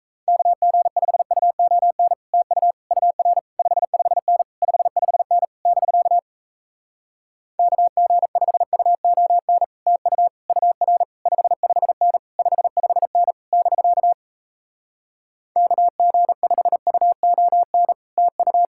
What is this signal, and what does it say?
Morse code, heard as a keyed tone, about 38 wpm, 700 Hz. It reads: KG5UON TU UR 55N 55N <BK> KG5UON TU UR 55N 55N <BK> KG5UON TU